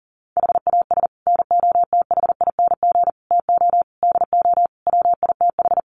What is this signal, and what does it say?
Morse code, heard as a keyed tone, about 40 wpm, 700 Hz. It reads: HAS NOTHING TO DO WITH
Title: Sets of 5 Words 40wpm